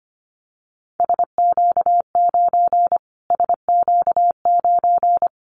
Morse code, heard as a keyed tone, about 25 wpm, 700 Hz. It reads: SQ9 SQ9
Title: Callsign Prefixes for Top 25 DXCC - Repeated 3x - 25wpm